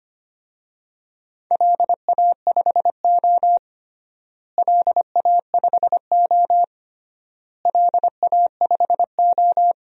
Morse code, heard as a keyed tone, about 25 wpm, 700 Hz. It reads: LA5O LA5O LA5O
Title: DXCC-100 Call Signs - 4 Characters - Repeated 3x - 25wpm